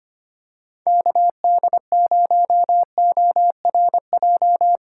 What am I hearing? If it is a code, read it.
Morse code, heard as a keyed tone, about 25 wpm, 700 Hz. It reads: KD0ORJ
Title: USA Call Signs - 6 Characters - 4x Character Spacing - 25wpm